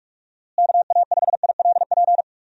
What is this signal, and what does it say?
Morse code, heard as a keyed tone, about 46 wpm, 700 Hz. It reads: KA5ILP
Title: USA Call Signs - 6 Characters - Repeated 2x - 45wpm